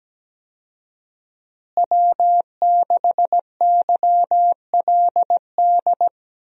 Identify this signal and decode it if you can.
Morse code, heard as a keyed tone, about 17 wpm, 700 Hz. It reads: W6YLD
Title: USA Call Signs - 5 Characters - Repeated 3x - 17wpm